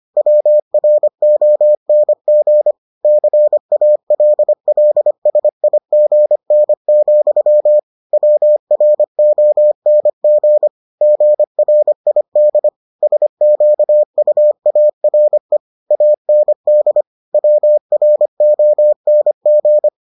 Morse code, heard as a keyed tone, about 25 wpm, 600 Hz. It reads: WRONG CALLSIGN, WRONG GRID SQUARE AND WRONG